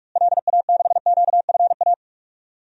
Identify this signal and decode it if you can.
Morse code, heard as a keyed tone, about 43 wpm, 700 Hz. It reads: LA6QFA
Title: DXCC-100 Call Signs - 6 Characters - Repeated 2x - 45wpm